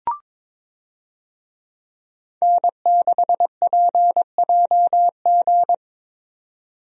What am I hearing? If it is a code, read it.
Morse code, heard as a keyed tone, about 22 wpm, 700 Hz. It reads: N6PJG
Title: USA Call Signs - 5 Characters - 22wpm